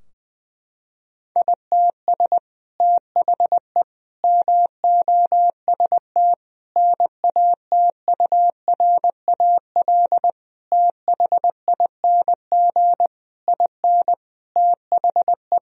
Morse code, heard as a keyed tone, about 20 wpm, 700 Hz. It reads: ITS THE MOST NATURAL THING IN THE
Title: Sentences from Top 1000 Words - Review Part 4 - 20wpm